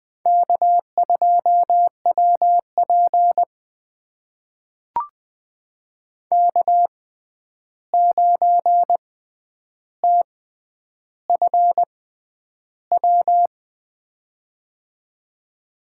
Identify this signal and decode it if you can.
Morse code, heard as a keyed tone, about 20 wpm, spaced as for 10 wpm, 700 Hz. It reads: K2WP K9TFW
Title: USA Call Signs - 6x Character Spacing - 20wpm